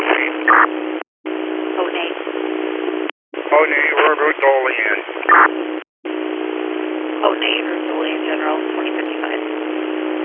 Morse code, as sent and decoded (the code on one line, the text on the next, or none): none